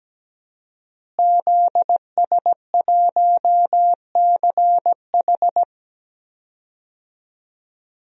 Morse code, heard as a keyed tone, about 17 wpm, 700 Hz. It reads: ZS1CH
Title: DXCC-100 Call Signs - 5 Characters - Repeated 3x - 17wpm